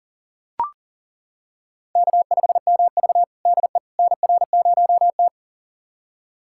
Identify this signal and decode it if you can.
Morse code, heard as a keyed tone, about 38 wpm, 700 Hz. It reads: K5MV DE NR0T
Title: USA Callsign DE Callsign - 4 Characters - 40wpm